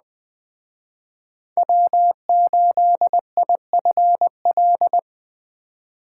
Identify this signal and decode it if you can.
Morse code, heard as a keyed tone, about 21 wpm, 700 Hz. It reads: W8IFL